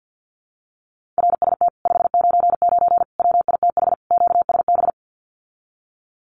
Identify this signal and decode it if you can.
Morse code, heard as a keyed tone, about 48 wpm, 700 Hz. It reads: RST 599 WITH QSB
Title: Common QSO Phrases Encore 50wpm